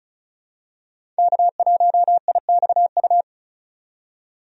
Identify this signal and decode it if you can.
Morse code, heard as a keyed tone, about 34 wpm, 700 Hz. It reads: K1IXU
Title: USA Call Signs - 5 Characters - 35wpm